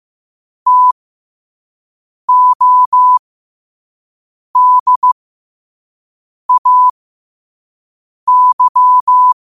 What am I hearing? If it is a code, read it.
Morse code, heard as a keyed tone, about 15 wpm, spaced as for 5 wpm, 1000 Hz. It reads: TODAY
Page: New quotes every day in morse code at 5 Words per minute.